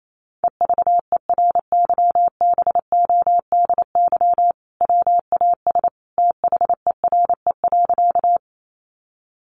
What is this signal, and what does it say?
Morse code, heard as a keyed tone, about 28 wpm, 700 Hz. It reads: EVERYBODY WAS THERE.
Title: Sentences from Top 1400 Words 28wpm